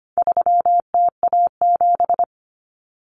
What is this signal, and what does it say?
Morse code, heard as a keyed tone, about 25 wpm, 700 Hz. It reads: 3TA7
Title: Common Cut Numbers - 4-Digit 25wpm